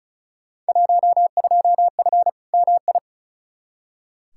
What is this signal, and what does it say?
Morse code, heard as a keyed tone, about 36 wpm, 700 Hz. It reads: 12F MI
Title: Field Day Exchanges Encore - 35wpm